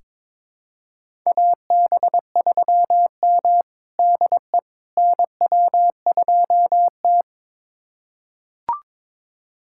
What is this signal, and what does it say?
Morse code, heard as a keyed tone, about 22 wpm, 700 Hz. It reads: AB3M DE NW2T
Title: USA Callsign DE Callsign - 4 Characters - 22wpm